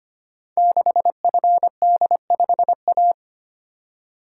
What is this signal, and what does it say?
Morse code, heard as a keyed tone, about 25 wpm, 700 Hz. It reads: 6FD5A
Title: Code Groups - 5 Letter:Numbers - 25wpm